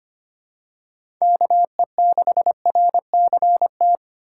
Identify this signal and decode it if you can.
Morse code, heard as a keyed tone, about 25 wpm, 700 Hz. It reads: KE6RCT
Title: USA Call Signs - 3x Character Spacing - 25wpm